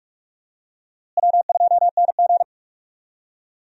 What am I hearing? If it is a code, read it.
Morse code, heard as a keyed tone, about 42 wpm, 700 Hz. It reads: W2NG
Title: USA Call Signs - 4 Characters - Repeated 3x - 45wpm